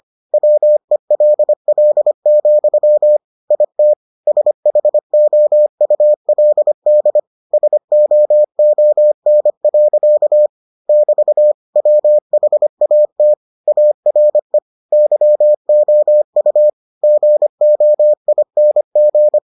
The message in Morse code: WELL, IT SHOULD SOON. <BT> WHAT ARE YOU GOING

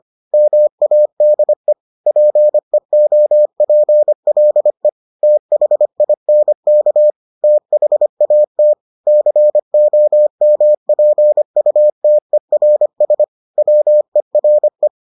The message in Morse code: MADE PEOPLE THINK THAT COMPUTERS WERE

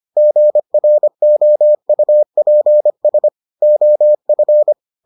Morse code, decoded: GROUPS OF

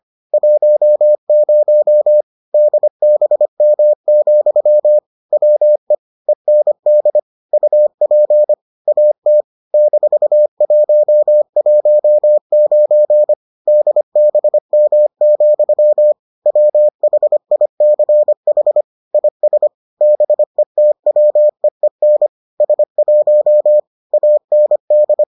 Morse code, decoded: 10 DBM, WE END UP AT -119 DBM, WHICH IS BETWEEN S1 AND

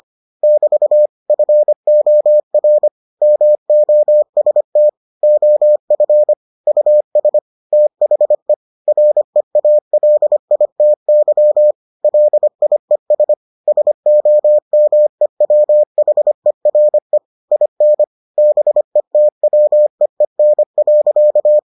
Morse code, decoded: <BT> FOR MOST OF US THE REALITY LIES SOMEWHERE IN BETWEEN.

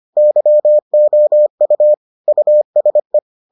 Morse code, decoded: YOU USE

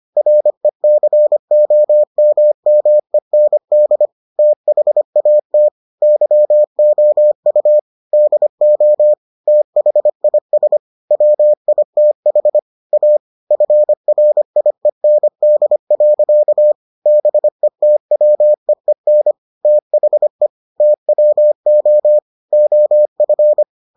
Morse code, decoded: RECOMMEND THAT YOU DO THIS WITH A FRIEND. BETWEEN THE TWO OF

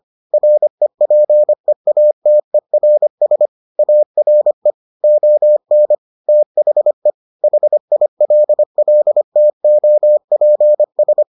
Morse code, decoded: REPEATERS ARE ON THE HILLTOPS

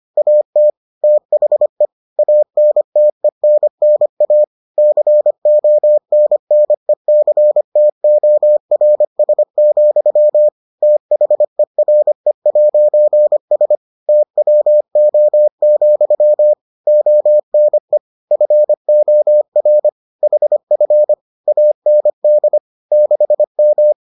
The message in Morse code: AT THE ANTENNA CONNECTORS, THERE'S TWO, ONE FOR HF AND 6M

